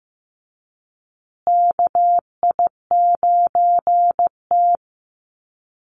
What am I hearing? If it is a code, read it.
Morse code, heard as a keyed tone, about 15 wpm, 700 Hz. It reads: KI9T
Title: USA Call Signs Repeated 3x 15wpm